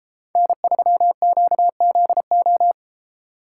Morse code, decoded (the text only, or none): N3QZO